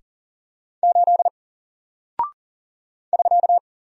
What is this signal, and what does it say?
Morse code, heard as a keyed tone, about 40 wpm, 700 Hz. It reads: 8 <SK>
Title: Single Character (Letter:Number:Punctuation:Prosign) Alt 40wpm